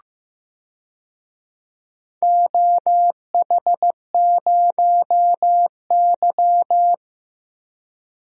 Morse code, heard as a keyed tone, about 15 wpm, 700 Hz. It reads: OH0Y
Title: DXCC-100 Call Signs - 4 Characters - 15wpm